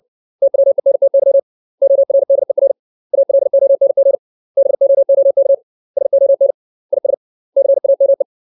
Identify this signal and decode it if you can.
Morse code, heard as a keyed tone, about 60 wpm, 550 Hz. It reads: TWENTY ORDER ALONG BOOK SON IS CAME
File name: 632 words / 632 Words 60WPM